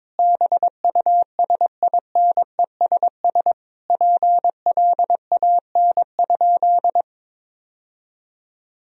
Morse code, heard as a keyed tone, about 22 wpm, 700 Hz. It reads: BUSINESS PLAN?
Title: Sentences from Top 1000 Words - Review Part 3 - 22wpm